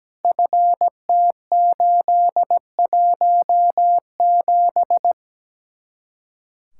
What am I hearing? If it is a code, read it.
Morse code, heard as a keyed tone, about 17 wpm, 700 Hz. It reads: FT817
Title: Speed-Racing Common QSO Phrases - 17wpm